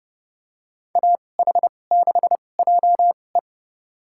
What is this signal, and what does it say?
Morse code, heard as a keyed tone, about 30 wpm, spaced as for 22 wpm, 700 Hz. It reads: AH6JE
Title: USA Call Signs - 5 Characters - 2x Character Spacing - 30wpm